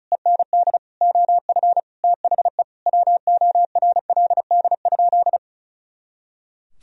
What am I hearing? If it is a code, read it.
Morse code, heard as a keyed tone, about 34 wpm, 700 Hz. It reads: END OF THE WORLD?
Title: Sentences from Top 200 Words and Abbreviations 35wpm